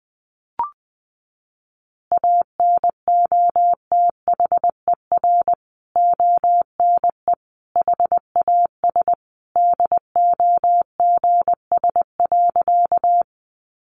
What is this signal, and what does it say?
Morse code, heard as a keyed tone, about 20 wpm, 700 Hz. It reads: ANOTHER ONE HAS DOGS.
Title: Additional Easy Phrases 20wpm